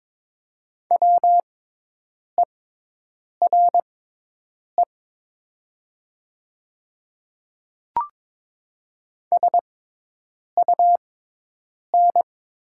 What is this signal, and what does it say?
Morse code, heard as a keyed tone, about 22 wpm, spaced as for 8 wpm, 700 Hz. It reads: WERE SUN
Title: 6x Character Spacing - Top 300 Words - 22wpm